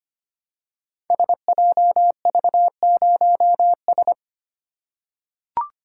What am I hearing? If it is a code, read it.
Morse code, heard as a keyed tone, about 25 wpm, 700 Hz. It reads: SJV0S
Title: Code Groups - 5 Letter:Numbers - 25wpm